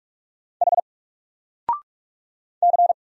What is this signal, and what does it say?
Morse code, heard as a keyed tone, about 44 wpm, 700 Hz. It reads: H C
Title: Single Letter-Number 45wpm